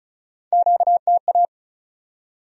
Morse code, heard as a keyed tone, about 50 wpm, spaced as for 42 wpm, 700 Hz. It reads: QTA
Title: Speed-Racing Q-Codes - 35wpm